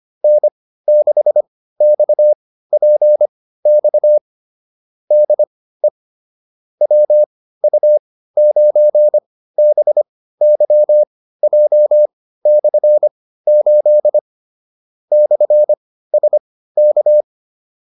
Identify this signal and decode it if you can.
Morse code, heard as a keyed tone, about 25 wpm, spaced as for 15 wpm, 600 Hz. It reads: N6XPX DE WU9BYJ/8 /SK